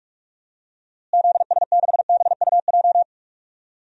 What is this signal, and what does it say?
Morse code, heard as a keyed tone, about 43 wpm, 700 Hz. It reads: ZS6BUJ
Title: DXCC-100 Call Signs - 6 Characters - Repeated 3x - 45wpm